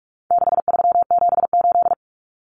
Morse code, heard as a keyed tone, about 44 wpm, 700 Hz. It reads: 6378
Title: Numbers - 4-Digit 45wpm